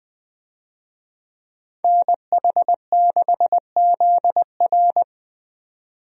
Morse code, heard as a keyed tone, about 20 wpm, 700 Hz. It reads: NH6ZR
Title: USA Call Signs - 5 Characters - 20wpm